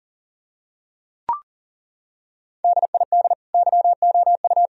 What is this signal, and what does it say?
Morse code, heard as a keyed tone, about 40 wpm, 700 Hz. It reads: DID YOU
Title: Sentences from Top 1000 Words - Review Part 4 - 40wpm